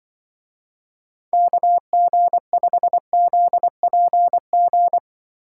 Morse code, heard as a keyed tone, about 23 wpm, 700 Hz. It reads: KG5ZPG
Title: Speed Racing - USA 2x3 Call Signs 20wpm